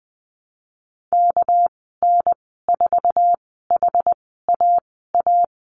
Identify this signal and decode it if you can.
Morse code, heard as a keyed tone, about 20 wpm, spaced as for 14 wpm, 700 Hz. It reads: KN4HAA
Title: 245 Lesson 39g - Letter z - Call Signs 20wpm